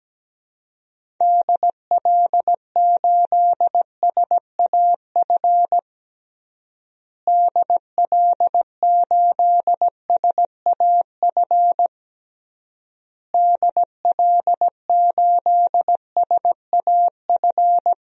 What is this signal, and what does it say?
Morse code, heard as a keyed tone, about 17 wpm, 700 Hz. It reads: DL8SAF DL8SAF DL8SAF
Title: DXCC-100 Call Signs - 6 Characters - Repeated 3x - 17wpm